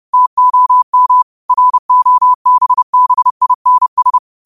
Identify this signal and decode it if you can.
Morse code, heard as a keyed tone, about 30 wpm, 1000 Hz. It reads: TOM ROBBINS
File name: qotdfiles / 30wpm